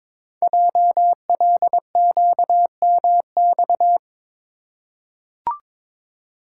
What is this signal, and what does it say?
Morse code, heard as a keyed tone, about 22 wpm, 700 Hz. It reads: JLQMX
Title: Code Groups - 5 Letters - 22wpm